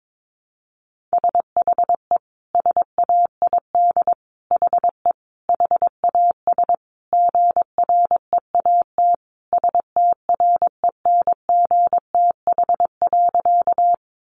Morse code, heard as a keyed tone, about 22 wpm, 700 Hz. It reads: SHE SAID HE HAS GREAT STRENGTH.
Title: Sentences from Top 1300 Words 22wpm